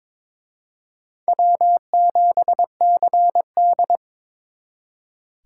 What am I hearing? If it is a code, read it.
Morse code, heard as a keyed tone, about 26 wpm, 700 Hz. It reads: W7CD
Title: Speed Racing - USA 1x2 Call Signs 22wpm